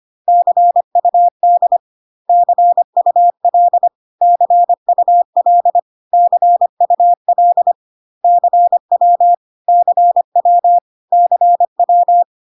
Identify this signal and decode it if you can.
Morse code, heard as a keyed tone, about 25 wpm, 700 Hz. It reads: CUD CUL CUL CUL CW CW CW
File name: cw / CW common abbreviations alphabetical repeated 3 times 25 25wpm ik1hge